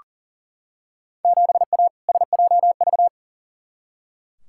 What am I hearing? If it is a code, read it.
Morse code, heard as a keyed tone, about 40 wpm, 700 Hz. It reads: 7A SJV
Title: Field Day Exchanges Encore - 40wpm